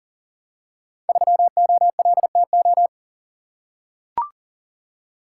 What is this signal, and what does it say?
Morse code, heard as a keyed tone, about 42 wpm, 700 Hz. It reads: 3OLTO
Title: Code Groups - 5 Letter:Numbers - 40wpm